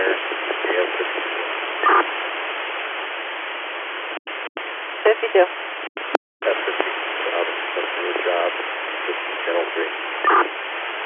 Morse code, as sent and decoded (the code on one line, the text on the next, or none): none